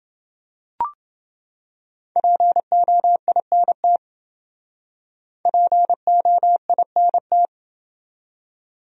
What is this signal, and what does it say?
Morse code, heard as a keyed tone, about 28 wpm, 700 Hz. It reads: POINT POINT